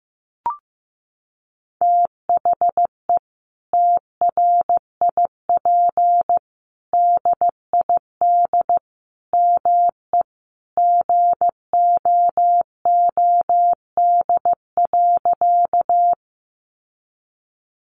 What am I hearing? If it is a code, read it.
Morse code, heard as a keyed tone, about 15 wpm, 700 Hz. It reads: THE TRIP DID ME GOOD.
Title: Sentences from Top 1300 Words 15wpm